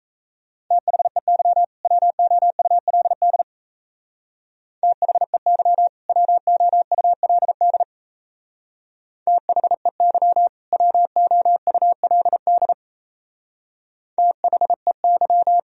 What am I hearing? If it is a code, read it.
Morse code, heard as a keyed tone, about 36 wpm, 700 Hz. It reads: THEY WOULD THEY WOULD THEY WOULD THEY